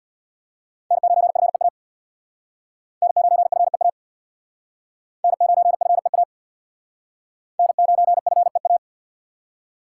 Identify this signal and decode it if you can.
Morse code, heard as a keyed tone, about 62 wpm, 700 Hz. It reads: N9FEA N9FEA N9FEA N9FEA
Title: Speed Racing - USA 1x3 Call Signs 50wpm